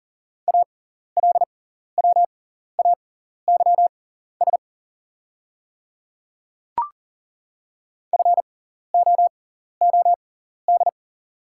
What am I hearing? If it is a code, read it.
Morse code, heard as a keyed tone, about 40 wpm, spaced as for 14 wpm, 700 Hz. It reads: ALWAYS FOOD
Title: 6x Character Spacing - Top 300 Words - 40wpm